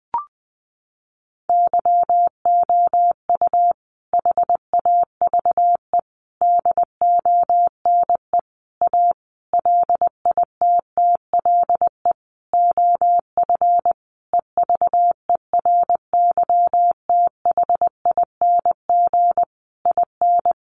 Morse code, heard as a keyed tone, about 19 wpm, 700 Hz. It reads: YOU HAVE DONE A LITTLE OF EVERYTHING IN
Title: Sentences from Top 700 Words 20wpm